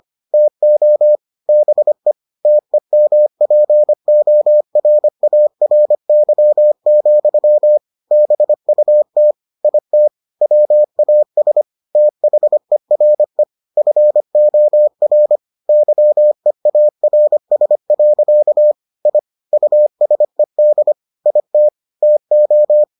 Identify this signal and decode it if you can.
Morse code, heard as a keyed tone, about 25 wpm, 600 Hz. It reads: TO BE TEMPORARY, BUT IT WAS THERE FOR YEARS. I USED IT TO